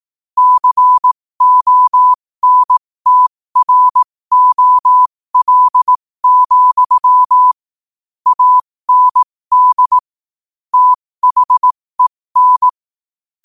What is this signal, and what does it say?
Morse code, heard as a keyed tone, about 18 wpm, spaced as for 15 wpm, 1000 Hz. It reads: CONTROL, AND THEN
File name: qotdfiles / 15wpm